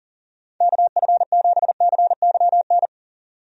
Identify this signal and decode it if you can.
Morse code, heard as a keyed tone, about 37 wpm, 700 Hz. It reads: KF7CYN
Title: USA Call Signs - 6 Characters - 40wpm